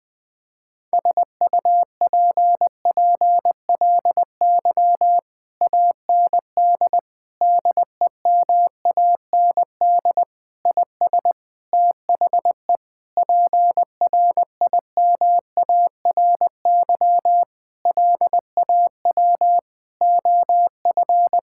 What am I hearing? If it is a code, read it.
Morse code, heard as a keyed tone, about 20 wpm, 700 Hz. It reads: SUPPLY AND DEMAND IS THE PRIMARY LAW OF
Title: Sentences from Top 1000 Words - Review Part 6 - 20wpm